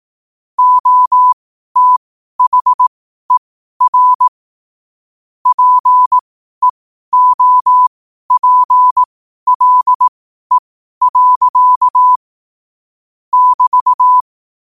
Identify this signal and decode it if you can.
Morse code, heard as a keyed tone, about 18 wpm, spaced as for 12 wpm, 1000 Hz. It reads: OTHER PEOPLE. <BT>
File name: qotdfiles / 12wpm